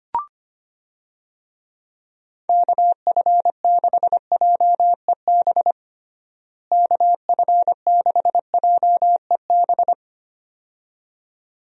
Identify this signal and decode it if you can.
Morse code, heard as a keyed tone, about 25 wpm, 700 Hz. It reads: KF6JEB KF6JEB
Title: USA Call Signs - 6 Characters - Repeated 2x - 25wpm